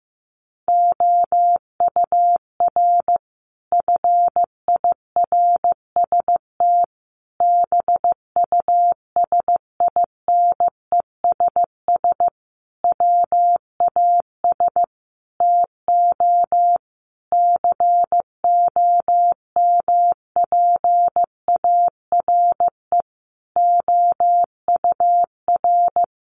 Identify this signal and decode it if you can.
Morse code, heard as a keyed tone, about 14 wpm, 700 Hz. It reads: OUR FIRST BUSINESS WAS TO COMPARE OUR
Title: Sentences from Top 1300 Words 15wpm